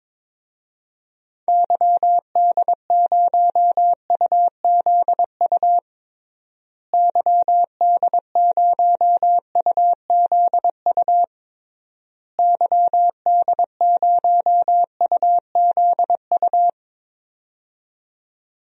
Morse code, heard as a keyed tone, about 22 wpm, 700 Hz. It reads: YD0UZU YD0UZU YD0UZU
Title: DXCC-100 Call Signs - 6 Characters - Repeated 3x - 22wpm